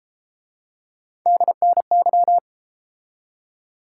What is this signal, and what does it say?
Morse code, heard as a keyed tone, about 30 wpm, 700 Hz. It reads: DNY